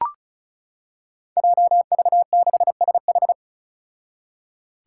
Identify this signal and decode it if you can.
Morse code, heard as a keyed tone, about 36 wpm, 700 Hz. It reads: JV6SH
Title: Five Challenging Letter-Number Combinations 35wpm